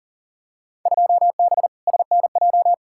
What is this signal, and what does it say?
Morse code, heard as a keyed tone, about 40 wpm, 700 Hz. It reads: 2B SNJ
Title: Field Day Exchanges Encore - 40wpm